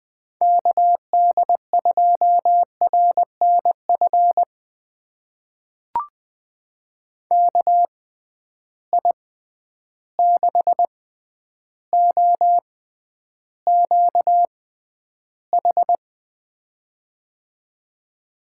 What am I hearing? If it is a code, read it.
Morse code, heard as a keyed tone, about 20 wpm, spaced as for 10 wpm, 700 Hz. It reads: KD2RNF KI6OQH
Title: USA Call Signs - 6 Characters - 6x Character Spacing - 20wpm